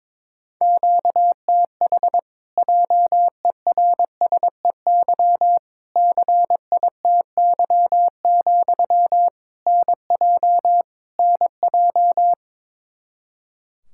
Morse code, heard as a keyed tone, about 22 wpm, 700 Hz. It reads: QTH JERSEY CITY, NJ NJ
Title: Practice QSO - Exchange 1 - Beginning - 22wpm